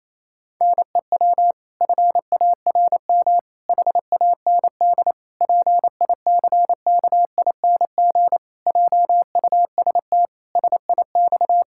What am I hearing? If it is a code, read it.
Morse code, heard as a keyed tone, about 27 wpm, 700 Hz. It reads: NEW FARM HAND PICKING JUST SIX
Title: Pangram Challenge v2 - 28wpm